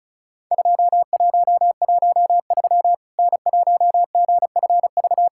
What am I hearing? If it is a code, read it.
Morse code, heard as a keyed tone, about 34 wpm, 700 Hz. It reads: 2113 N1GFV
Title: Worked All Europe DX Contest - QTC Exchange - 35wpm